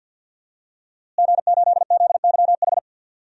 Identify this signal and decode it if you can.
Morse code, heard as a keyed tone, about 48 wpm, 700 Hz. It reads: G8ZYH
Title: DXCC-100 Call Signs - 5 Characters - Repeated 3x - 50wpm